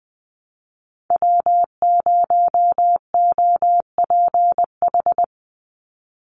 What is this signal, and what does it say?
Morse code, heard as a keyed tone, about 19 wpm, 700 Hz. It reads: W0OPH
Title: USA Call Signs 20wpm